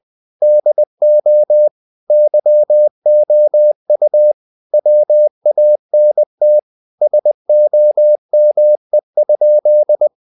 Morse code, heard as a keyed tone, about 20 wpm, 600 Hz. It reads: DO YOU WANT SOME?
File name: the old man and the sea / the old man and the sea part 1 20wpm